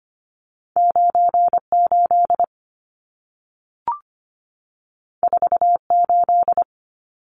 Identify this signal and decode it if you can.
Morse code, heard as a keyed tone, about 25 wpm, 700 Hz. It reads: 98 48
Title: Numbers - 2-Digit 25wpm